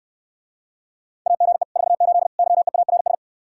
Morse code, heard as a keyed tone, about 69 wpm, 700 Hz. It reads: AGE 48 YEARS